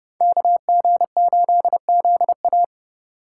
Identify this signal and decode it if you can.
Morse code, heard as a keyed tone, about 29 wpm, 700 Hz. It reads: KG8ZA